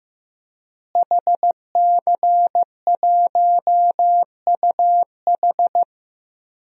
Morse code, heard as a keyed tone, about 15 wpm, 700 Hz. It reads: HC1UH
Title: DXCC-100 Call Signs - 5 Characters - Repeated 3x - 15wpm